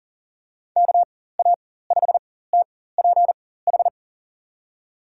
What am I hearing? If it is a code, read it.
Morse code, heard as a keyed tone, about 40 wpm, spaced as for 19 wpm, 700 Hz. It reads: KA5TPH
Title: USA Call Signs - 6 Characters - 4x Character Spacing - 40wpm